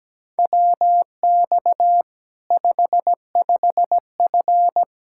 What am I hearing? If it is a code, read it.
Morse code, heard as a keyed tone, about 17 wpm, 700 Hz. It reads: WX 55F
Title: K1USN SST QSO Practice 17wpm